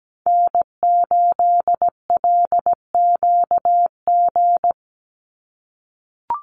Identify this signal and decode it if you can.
Morse code, heard as a keyed tone, about 17 wpm, 700 Hz. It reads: N8LQG
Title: USA Call Signs 17wpm